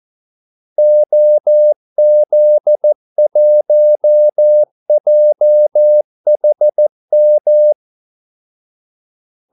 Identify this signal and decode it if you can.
Morse code, heard as a keyed tone, about 14 wpm, 600 Hz. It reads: OZ1JHM
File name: Callsigns-14wpm